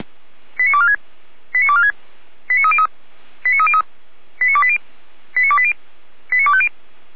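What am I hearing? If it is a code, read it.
Five-tone selective call (ZVEI), heard as quick runs of five tones, tones of about 70 ms. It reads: 89246 89246 89303 89303 89280 89280 89250
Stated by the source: ZVEI five-tone call 89235 received by SDR on channel 455, from the fire brigade server feuerwehr-orlamuende.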